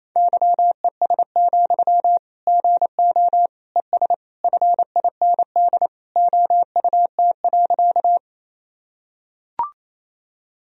Morse code, heard as a keyed tone, about 28 wpm, 700 Hz. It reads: YES, GO ES FIND OUT.
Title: Sentences from Top 500 Words and Abbreviations 28wpm